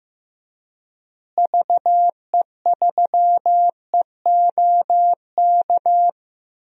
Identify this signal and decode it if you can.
Morse code, heard as a keyed tone, about 15 wpm, 700 Hz. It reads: VE3EOK
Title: DXCC-100 Call Signs - 6 Characters - Repeated 3x - 15wpm